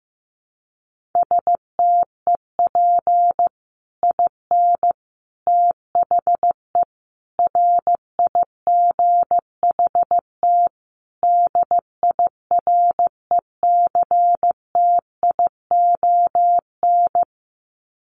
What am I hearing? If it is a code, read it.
Morse code, heard as a keyed tone, about 15 wpm, 700 Hz. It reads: STEP IN THE RIGHT DIRECTION
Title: Sets of 5 Words Encore 15wpm